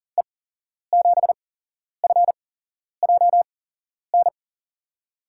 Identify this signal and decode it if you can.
Morse code, heard as a keyed tone, about 40 wpm, spaced as for 11 wpm, 700 Hz. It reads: E7FJN
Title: USA Call Signs - 6 Characters - 8x Character Spacing - 40wpm